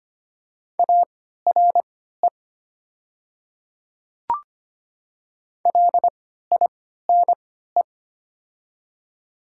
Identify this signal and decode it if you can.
Morse code, heard as a keyed tone, about 25 wpm, spaced as for 14 wpm, 700 Hz. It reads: ARE LINE
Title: 3x Character Spacing - Top 200 Words - 25wpm